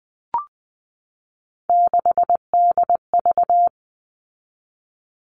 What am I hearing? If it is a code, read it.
Morse code, heard as a keyed tone, about 20 wpm, 700 Hz. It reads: 6DV
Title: All Cut Numbers - 3-Digit 20wpm